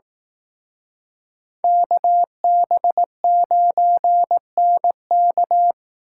Morse code, heard as a keyed tone, about 18 wpm, 700 Hz. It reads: KB9NK
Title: Speed Racing - USA 2x2 Call Signs 15wpm